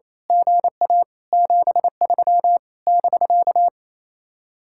Morse code, heard as a keyed tone, about 26 wpm, 700 Hz. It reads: GA 73 <BK>
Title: Speed-Racing Common QSO Phrases Encore - 28wpm